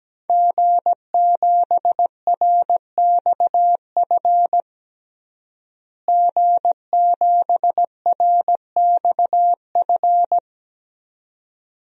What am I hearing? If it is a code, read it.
Morse code, heard as a keyed tone, about 17 wpm, 700 Hz. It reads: G7RXF G7RXF
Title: DXCC-100 Call Signs - 5 Characters - Repeated 2x - 17wpm